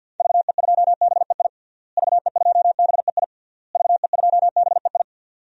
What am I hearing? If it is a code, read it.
Morse code, heard as a keyed tone, about 48 wpm, 700 Hz. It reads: VE2BEI VE2BEI VE2BEI
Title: DXCC-100 Call Signs - 6 Characters - Repeated 3x - 50wpm